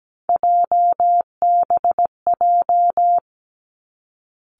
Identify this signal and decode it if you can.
Morse code, heard as a keyed tone, about 17 wpm, 700 Hz. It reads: JBJ
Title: Three Challenging Letter-Number Combinations 17wpm